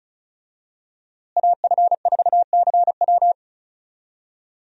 Morse code, heard as a keyed tone, about 36 wpm, 700 Hz. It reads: AF4CW
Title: USA Call Signs - 5 Characters - Repeated 2x - 35wpm